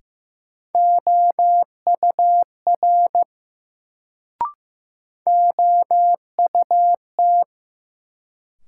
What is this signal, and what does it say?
Morse code, heard as a keyed tone, about 15 wpm, 700 Hz. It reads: OUR OUT
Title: Sets of 2 Words - New Word Drill - 15wpm